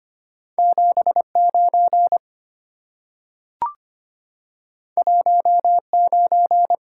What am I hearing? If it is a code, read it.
Morse code, heard as a keyed tone, about 25 wpm, 700 Hz. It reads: 79 19
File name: Two Letter-Number Combinations - 25wpm